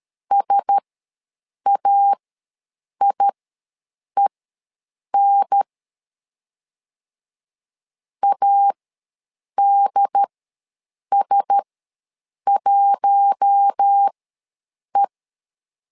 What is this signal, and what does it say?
Morse code, heard as a keyed tone, about 13 wpm, spaced as for 7 wpm, 800 Hz. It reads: SAIEN ADS1E
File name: LEZIONI / 4 Q1S ws 12wpm 1827